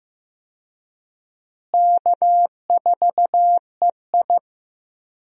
Morse code, heard as a keyed tone, about 15 wpm, 700 Hz. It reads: K4EI
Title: USA Call Signs - 4 Characters - Repeated 3x - 15wpm